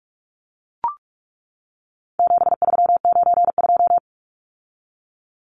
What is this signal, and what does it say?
Morse code, heard as a keyed tone, about 44 wpm, 700 Hz. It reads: 7392
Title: Numbers - 4-Digit 45wpm